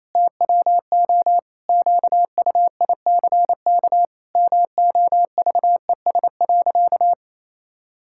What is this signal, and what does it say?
Morse code, heard as a keyed tone, about 28 wpm, 700 Hz. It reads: TWO QUICK MOVES.
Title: Pangram Challenge v2 - 28wpm